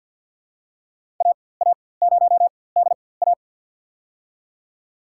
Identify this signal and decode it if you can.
Morse code, heard as a keyed tone, about 52 wpm, spaced as for 23 wpm, 700 Hz. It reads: AA0DA
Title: USA Call Signs - 5 Characters - 4x Character Spacing - 50wpm